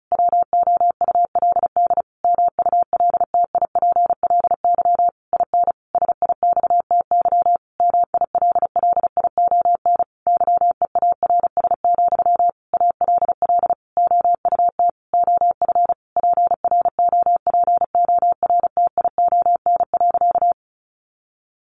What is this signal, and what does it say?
Morse code, heard as a keyed tone, about 34 wpm, 700 Hz. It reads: WOULD MULTIPLY IN SIXTY MILLION YEARS, ALL OUT OF PROPORTION.
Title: A Sound Of Thunder 35wpm